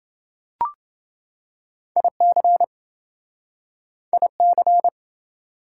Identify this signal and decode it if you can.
Morse code, heard as a keyed tone, about 27 wpm, 700 Hz. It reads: IC IC